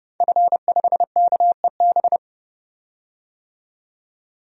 Morse code, heard as a keyed tone, about 30 wpm, 700 Hz. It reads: F5KEB
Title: DXCC-100 Call Signs - 5 Characters - 30wpm